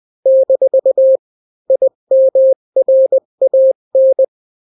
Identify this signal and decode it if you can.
Morse code, heard as a keyed tone, about 20 wpm, 530 Hz. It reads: - IMRAN
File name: daily / 18wpm